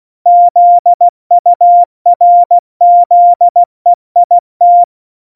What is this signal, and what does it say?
Morse code, heard as a keyed tone, about 16 wpm, 700 Hz. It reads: ZURZEIT
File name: Morsekurs / Witze / Witz 03. September 2019, 70 F80